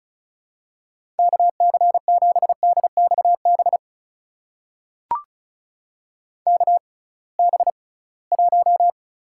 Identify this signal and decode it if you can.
Morse code, heard as a keyed tone, about 36 wpm, spaced as for 23 wpm, 700 Hz. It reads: KC7DXB KB1
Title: USA Call Signs - 6 Characters - 6x Character Spacing - 35wpm